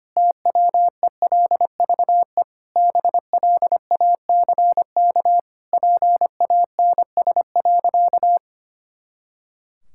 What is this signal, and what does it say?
Morse code, heard as a keyed tone, about 25 wpm, 700 Hz. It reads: TWELVE BLACK PANS.
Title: Pangram Challenge v2 - 25wpm